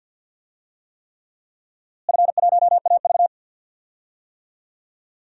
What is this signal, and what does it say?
Morse code, heard as a keyed tone, about 53 wpm, 700 Hz. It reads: F1AV
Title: DXCC-100 Call Signs - 4 Characters - Repeated 3x - 50wpm